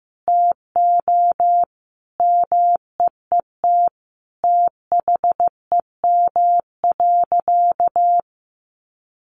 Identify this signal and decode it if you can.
Morse code, heard as a keyed tone, about 15 wpm, 700 Hz. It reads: TO MEET THEM.
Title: Sentences from Top 1100 Words 15wpm